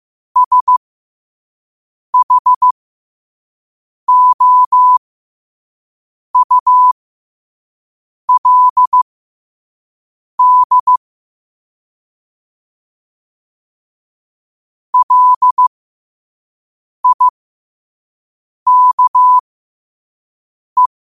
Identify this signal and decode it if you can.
Morse code, heard as a keyed tone, about 15 wpm, spaced as for 5 wpm, 1000 Hz. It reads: SHOULD LIKE
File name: qotdfiles / 5wpm